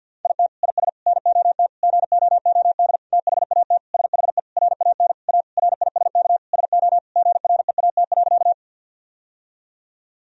Morse code, heard as a keyed tone, about 50 wpm, 700 Hz. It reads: IT IS NOT GOOD THAT SHE RAN A RISK SO GREAT.
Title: Sentences from Top 1000 Words - Review Part 4 - 50wpm